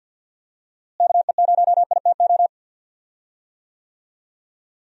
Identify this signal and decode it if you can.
Morse code, heard as a keyed tone, about 47 wpm, 700 Hz. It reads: KE0ITO
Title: USA Call Signs - 6 Characters - Repeated 2x - 50wpm